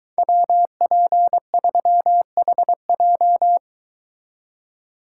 Morse code, heard as a keyed tone, about 22 wpm, 700 Hz. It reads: WP3HJ